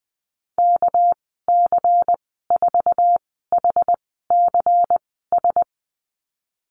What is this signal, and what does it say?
Morse code, heard as a keyed tone, about 20 wpm, spaced as for 15 wpm, 700 Hz. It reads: KC4HCS
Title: 207 Lesson 33g - Letter j - Call Signs 20wpm